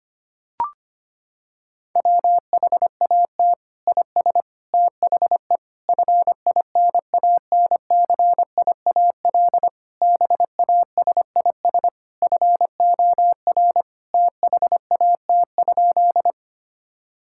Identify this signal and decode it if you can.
Morse code, heard as a keyed tone, about 25 wpm, 700 Hz. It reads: WHAT IS THE FINANCIAL BASIS FOR THAT?
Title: Sentences from Top 1000 Words - Review Part 5 - 25wpm